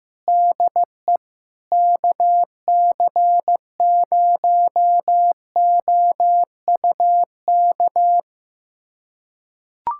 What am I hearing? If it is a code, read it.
Morse code, heard as a keyed tone, about 16 wpm, 700 Hz. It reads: DE KC0OUK
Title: USA Callsign DE Callsign - 6 Characters - 15wpm